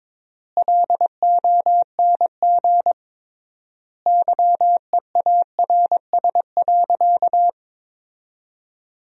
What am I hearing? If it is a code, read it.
Morse code, heard as a keyed tone, about 22 wpm, 700 Hz. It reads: LONG YEARS.
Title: 3x Word-Spacing - Sentences from Top 200 Words - 22wpm